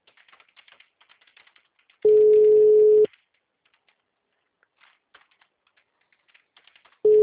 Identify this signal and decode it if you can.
Ringback tone (Europe, 425 Hz).